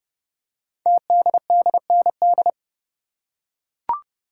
Morse code, heard as a keyed tone, about 30 wpm, 700 Hz. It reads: TDDND
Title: Code Groups - 5 Letters - 30wpm